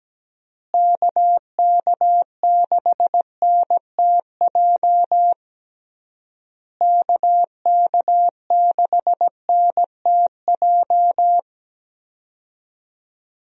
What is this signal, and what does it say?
Morse code, heard as a keyed tone, about 17 wpm, 700 Hz. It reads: KK6NTJ KK6NTJ
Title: USA Call Signs - 6 Characters - Repeated 3x - 17wpm